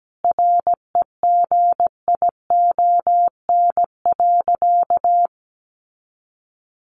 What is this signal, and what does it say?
Morse code, heard as a keyed tone, about 17 wpm, 700 Hz. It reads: REGION.
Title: Sentences from Top 1500 Words 17wpm